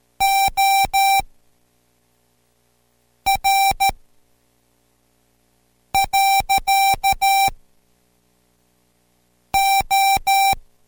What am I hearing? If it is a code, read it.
Morse code, heard as a keyed tone, about 13 wpm, spaced as for 4 wpm, 780 Hz. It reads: OR. O